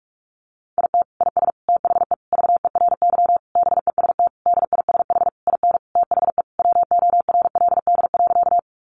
Morse code, heard as a keyed tone, about 44 wpm, 700 Hz. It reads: IT IS THE VERY BEST DISH IN THE WORLD.
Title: Sentences from Top 1300 Words 45wpm